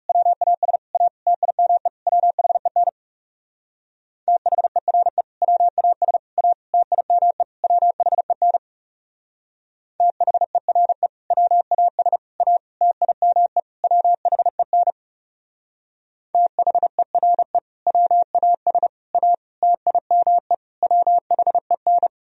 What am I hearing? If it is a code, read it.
Morse code, heard as a keyed tone, about 36 wpm, 700 Hz. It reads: WAS A TIME WHEN THERE WAS A TIME WHEN THERE WAS A TIME WHEN THERE WAS A TIME WHEN
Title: Speed-Racing - Sets of 5 Words 30wpm